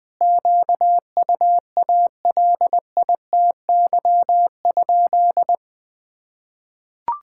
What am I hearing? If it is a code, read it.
Morse code, heard as a keyed tone, about 21 wpm, 700 Hz. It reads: QUALITY?
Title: Sentences from Top 1000 Words - Review Part 6 - 20wpm